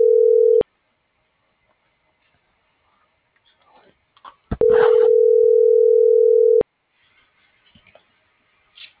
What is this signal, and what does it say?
Ringback tone (North America), 440 + 480 Hz.